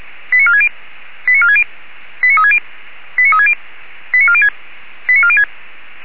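Five-tone selective call (ZVEI), heard as quick runs of five tones, tones of about 80 ms. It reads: 89460 89460 89360 89360 89406 89406
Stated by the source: ZVEI five-tone selective call received by SDR.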